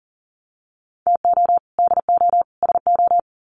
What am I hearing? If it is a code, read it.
Morse code, heard as a keyed tone, about 40 wpm, 700 Hz. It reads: TO DO SO
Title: Sets of 3 Words 40wpm